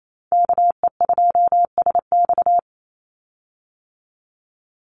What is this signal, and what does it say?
Morse code, heard as a keyed tone, about 28 wpm, 700 Hz. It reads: KE2SX
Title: Call Signs Repeated 2x - 28wpm